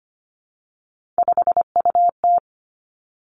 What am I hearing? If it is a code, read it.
Morse code, heard as a keyed tone, about 25 wpm, 700 Hz. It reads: 5UT
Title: Three Challenging Letter-Number Combinations 25wpm